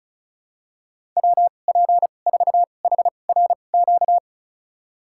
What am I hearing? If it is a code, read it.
Morse code, heard as a keyed tone, about 34 wpm, spaced as for 25 wpm, 700 Hz. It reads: WP4HRQ
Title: USA Call Signs - 2x Character Spacing - 35wpm